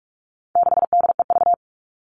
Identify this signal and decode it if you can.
Morse code, heard as a keyed tone, about 44 wpm, 700 Hz. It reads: 6DEV
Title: All Cut Numbers - 4-Digit 45wpm